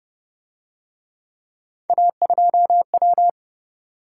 Morse code, heard as a keyed tone, about 26 wpm, 700 Hz. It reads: A2W